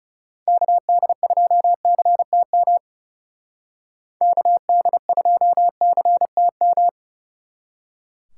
Morse code, heard as a keyed tone, about 39 wpm, 700 Hz. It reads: KD2CTM KD2CTM